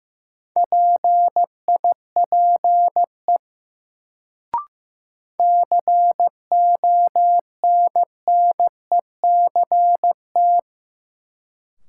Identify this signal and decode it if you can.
Morse code, heard as a keyed tone, about 15 wpm, 700 Hz. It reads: PIPE CONNECT
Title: Sentences from Top 1500 Words - New Word Drill - 15wpm